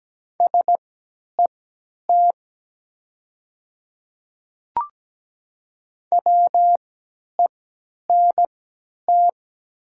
Morse code, heard as a keyed tone, about 17 wpm, spaced as for 10 wpm, 700 Hz. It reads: SET WENT
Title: 3x Character Spacing - Top 300 Words - 17wpm